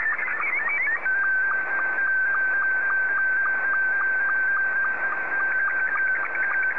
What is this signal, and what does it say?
SSTV (slow-scan television), Scottie 2.
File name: sstv / S2-JA7QJQ